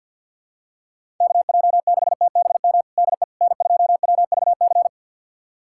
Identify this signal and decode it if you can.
Morse code, heard as a keyed tone, about 51 wpm, 700 Hz. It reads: KJ6TBM DE N2WVC
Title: USA Callsign DE Callsign 50wpm